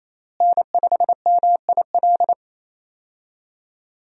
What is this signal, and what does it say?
Morse code, heard as a keyed tone, about 28 wpm, 700 Hz. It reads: N5MIL
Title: USA Call Signs - 5 Characters - Repeated 3x - 28wpm